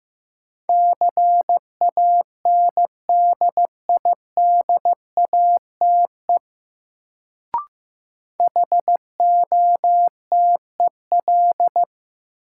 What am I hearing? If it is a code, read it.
Morse code, heard as a keyed tone, about 15 wpm, 700 Hz. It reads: CANDIDATE HOTEL
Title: Sentences from Top 1600 Words - New Word Drill Alt - 15wpm